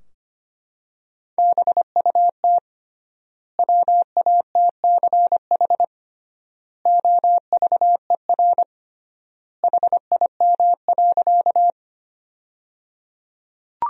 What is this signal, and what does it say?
Morse code, heard as a keyed tone, about 25 wpm, 700 Hz. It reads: BUT WATCH OVER HIM.
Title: 3x Word-Spacing - Sentences from Top 300 Words - 25wpm